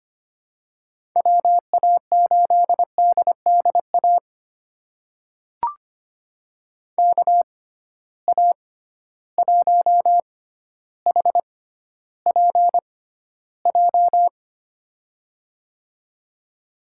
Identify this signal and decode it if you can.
Morse code, heard as a keyed tone, about 25 wpm, spaced as for 13 wpm, 700 Hz. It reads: WA8DDA KA1HPJ
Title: USA Call Signs - 6 Characters - 6x Character Spacing - 25wpm